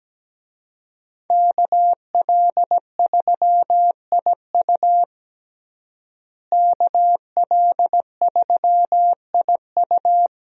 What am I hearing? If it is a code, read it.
Morse code, heard as a keyed tone, about 17 wpm, 700 Hz. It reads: KL3IU KL3IU
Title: USA Call Signs - 5 Characters - Repeated 3x - 17wpm